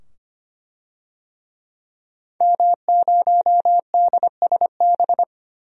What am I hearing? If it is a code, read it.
Morse code, heard as a keyed tone, about 25 wpm, 700 Hz. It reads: M0DSB
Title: DXCC-100 Call Signs - 5 Characters - 25wpm